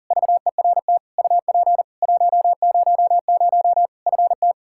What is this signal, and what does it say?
Morse code, heard as a keyed tone, about 40 wpm, 700 Hz. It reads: VERT UP 100 FT
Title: Practice QSO - Exchange 2 - Info - 40wpm